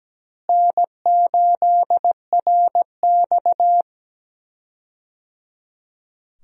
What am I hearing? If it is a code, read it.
Morse code, heard as a keyed tone, about 17 wpm, 700 Hz. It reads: N8RX
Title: USA Call Signs - 4 Characters - Repeated 3x - 17wpm